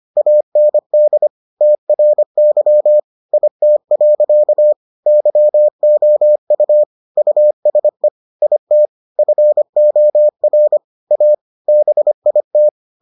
Morse code, heard as a keyed tone, about 25 wpm, 600 Hz. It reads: AND TRY IT. YOU USE IT FOR A BIT